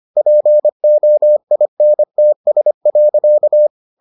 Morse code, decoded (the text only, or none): POINTS.